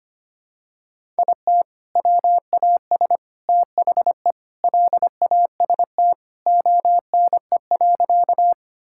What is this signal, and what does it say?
Morse code, heard as a keyed tone, about 25 wpm, 700 Hz. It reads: IT WAS THE LAST ONE.
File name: practice / 25wpm / Sentences with 3-Word Phrases - 25wpm